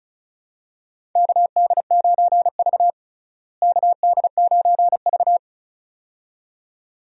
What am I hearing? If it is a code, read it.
Morse code, heard as a keyed tone, about 34 wpm, 700 Hz. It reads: KD9V KD9V
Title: USA Call Signs - 4 Characters - Repeated 2x - 35wpm